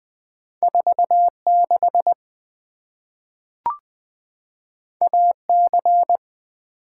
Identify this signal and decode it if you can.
Morse code, heard as a keyed tone, about 20 wpm, 700 Hz. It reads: 46 AC
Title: Two Letter-Number Combinations - 20wpm